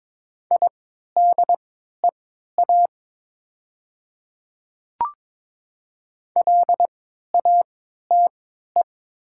Morse code, heard as a keyed tone, about 22 wpm, spaced as for 12 wpm, 700 Hz. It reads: IDEA LATE
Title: 3x Character Spacing - Top 300 Words - 22wpm